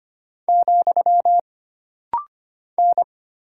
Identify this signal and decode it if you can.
Morse code, heard as a keyed tone, about 25 wpm, 700 Hz. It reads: , N
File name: Single Character (Letter:Number:Punctuation:Prosign) - Rapid-Fire - 25wpm